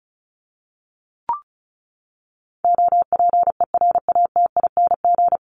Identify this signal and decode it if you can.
Morse code, heard as a keyed tone, about 34 wpm, 700 Hz. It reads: OPERATING
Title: ING Suffix - Top 100 Words with Base Words 35wpm